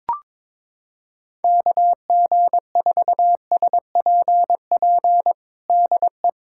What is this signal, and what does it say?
Morse code, heard as a keyed tone, about 22 wpm, 700 Hz. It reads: KG4SPP DE
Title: USA Callsign DE Callsign - 6 Characters - 22wpm